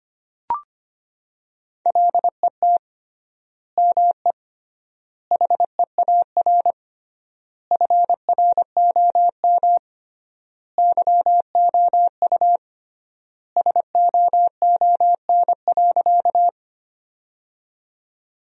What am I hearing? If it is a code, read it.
Morse code, heard as a keyed tone, about 25 wpm, 700 Hz. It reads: LET ME HEAR FROM YOU SOON.
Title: 3x Word-Spacing - Sentences from Top 400 Words - 25wpm